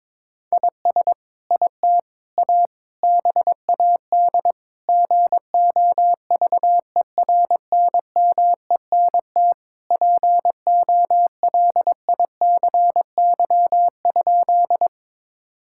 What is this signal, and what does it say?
Morse code, heard as a keyed tone, about 22 wpm, 700 Hz. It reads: IS IT A BAD GOVERNMENT POLICY?
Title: Sentences from Top 1000 Words - Review Part 5 - 22wpm